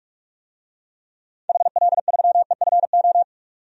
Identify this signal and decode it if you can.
Morse code, heard as a keyed tone, about 46 wpm, 700 Hz. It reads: HL3EFO
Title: DXCC-100 Call Signs - 6 Characters - 45wpm